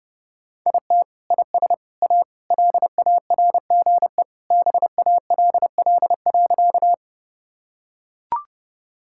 Morse code, heard as a keyed tone, about 30 wpm, 700 Hz. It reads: IT IS A LARGE BALL.
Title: Sentences from Top 1000 Words - Review Part 6 - 30wpm